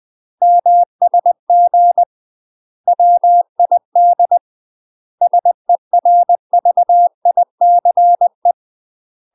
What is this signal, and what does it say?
Morse code, heard as a keyed tone, about 20 wpm, 700 Hz. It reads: MSG WID SERVICE
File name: UK MASTER QSOs TRAINING / 20WPM / UK MASTER QSOs 0012